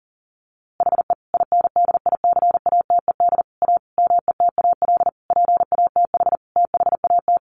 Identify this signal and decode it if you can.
Morse code, heard as a keyed tone, about 40 wpm, 700 Hz. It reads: HE INDICATED A METAL PATH THAT
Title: A Sound Of Thunder 40wpm